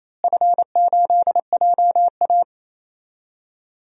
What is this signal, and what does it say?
Morse code, heard as a keyed tone, about 28 wpm, 700 Hz. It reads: F8JA
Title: DXCC-100 Call Signs - 4 Characters - Repeated 3x - 28wpm